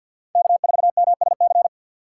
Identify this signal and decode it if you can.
Morse code, heard as a keyed tone, about 53 wpm, 700 Hz. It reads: K4MSC
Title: USA Call Signs - 5 Characters - 2x Character Spacing - 50wpm